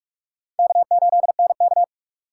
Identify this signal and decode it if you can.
Morse code, heard as a keyed tone, about 45 wpm, 700 Hz. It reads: K8NK
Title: USA Call Signs - 4 Characters - 8x Character Spacing - 45wpm